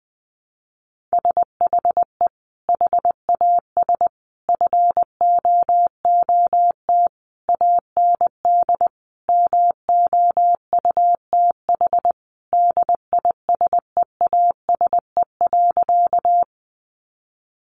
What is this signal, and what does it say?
Morse code, heard as a keyed tone, about 20 wpm, 700 Hz. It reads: SHE HAS FOOT AND MOUTH DISEASE.
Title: Sentences from Top 1500 Words 20wpm